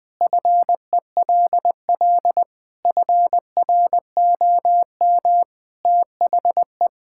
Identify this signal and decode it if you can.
Morse code, heard as a keyed tone, about 20 wpm, 700 Hz. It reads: FELL FROM THE
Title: Sentences with 2-Word Phrases - 20wpm